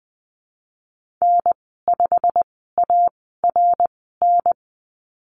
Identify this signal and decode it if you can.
Morse code, heard as a keyed tone, about 20 wpm, spaced as for 15 wpm, 700 Hz. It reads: N5ARN
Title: 200 Lesson 32g - Letter k - Call Signs 20wpm